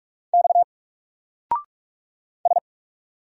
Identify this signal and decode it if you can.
Morse code, heard as a keyed tone, about 44 wpm, 700 Hz. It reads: X S
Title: Single Character (Letter:Number:Punctuation:Prosign) Alt 45wpm